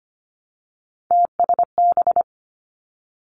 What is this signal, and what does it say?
Morse code, heard as a keyed tone, about 25 wpm, 700 Hz. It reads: TSB
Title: Three Challenging Letter-Number Combinations 25wpm